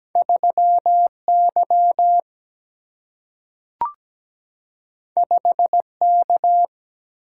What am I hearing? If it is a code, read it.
Morse code, heard as a keyed tone, about 17 wpm, 700 Hz. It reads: 3Y 5K
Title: Single-Letter Single-Number 17wpm